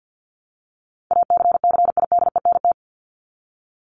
Morse code, heard as a keyed tone, about 48 wpm, 700 Hz. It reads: ACCIDENT
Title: Top 1751-2000 Words 50wpm